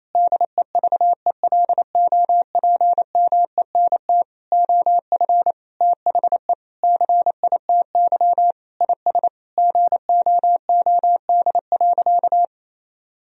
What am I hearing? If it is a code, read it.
Morse code, heard as a keyed tone, about 28 wpm, 700 Hz. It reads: DEVELOPMENT OF THE CITY IS GOOD.
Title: Sentences from Top 1000 Words - Review Part 4 - 28wpm